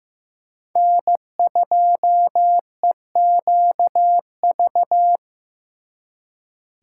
Morse code, heard as a keyed tone, about 18 wpm, 700 Hz. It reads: N2EQV
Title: Speed Racing - USA 1x3 Call Signs 15wpm